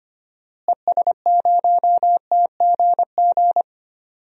Morse code, heard as a keyed tone, about 25 wpm, 700 Hz. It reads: ES0TGG
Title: DXCC-100 Call Signs - 6 Characters - Repeated 3x - 25wpm